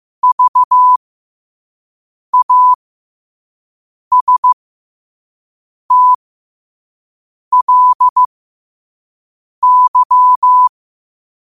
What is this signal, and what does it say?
Morse code, heard as a keyed tone, about 15 wpm, spaced as for 5 wpm, 1000 Hz. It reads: VASTLY
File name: 5wpm